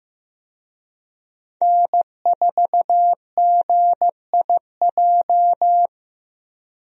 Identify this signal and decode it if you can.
Morse code, heard as a keyed tone, about 15 wpm, 700 Hz. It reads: N4GIJ
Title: USA Call Signs - 5 Characters - Repeated 2x - 15wpm